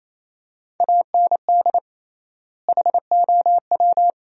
Morse code, heard as a keyed tone, about 28 wpm, 700 Hz. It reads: AND HOW
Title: 3x Word-Spacing - Sentences from Top 200 Words - 28wpm